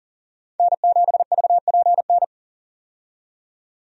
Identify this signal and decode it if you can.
Morse code, heard as a keyed tone, about 40 wpm, 700 Hz. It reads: N7VPN
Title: USA Call Signs - 5 Characters - 40wpm